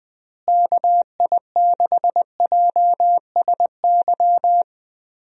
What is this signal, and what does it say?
Morse code, heard as a keyed tone, about 20 wpm, 700 Hz. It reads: KI6JSY
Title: USA Call Signs - 6 Characters - Repeated 2x - 20wpm